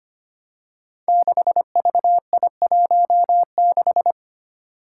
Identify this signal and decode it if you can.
Morse code, heard as a keyed tone, about 25 wpm, 700 Hz. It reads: 6VI16
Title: Five Challenging Letter-Number Combinations 25wpm